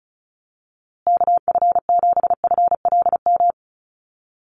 Morse code, heard as a keyed tone, about 34 wpm, 700 Hz. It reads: KF7FLM
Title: USA Call Signs Repeated 3x 35wpm